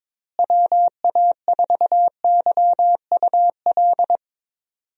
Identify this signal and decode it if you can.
Morse code, heard as a keyed tone, about 22 wpm, 700 Hz. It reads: WA4YUL
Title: USA Call Signs - 6 Characters - Repeated 3x - 22wpm